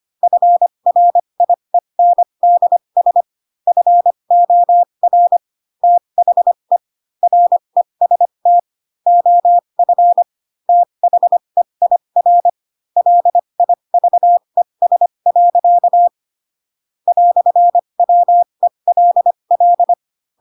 Morse code, heard as a keyed tone, about 25 wpm, spaced as for 22 wpm, 700 Hz. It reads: FRIENDS FOR THE REST OF THEIR LIVES. "WELL
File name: Pinocchio / Pinocchio ch2 18 25 ik1hge